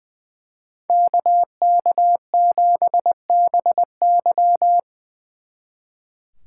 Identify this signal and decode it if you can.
Morse code, heard as a keyed tone, about 23 wpm, 700 Hz. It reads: KK7BY
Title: Speed Racing - USA 2x2 Call Signs 20wpm